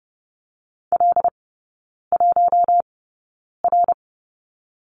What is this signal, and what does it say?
Morse code, heard as a keyed tone, about 30 wpm, spaced as for 9 wpm, 700 Hz. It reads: L1R
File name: course / 30:10wpm / 087 Lesson 15e - Number 5 - Three Characters 30wpm